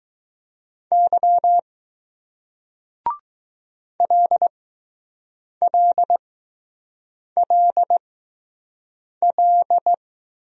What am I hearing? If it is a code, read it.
Morse code, heard as a keyed tone, about 20 wpm, spaced as for 6 wpm, 700 Hz. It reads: Y LLLL